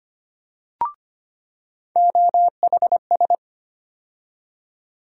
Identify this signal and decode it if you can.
Morse code, heard as a keyed tone, about 25 wpm, 700 Hz. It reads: OHS
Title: USA Call Sign Postfixes - 3 Letters 25wpm